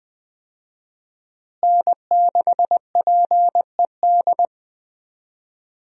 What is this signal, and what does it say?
Morse code, heard as a keyed tone, about 20 wpm, 700 Hz. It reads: N6PED
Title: USA Call Signs - 5 Characters - Repeated 2x - 20wpm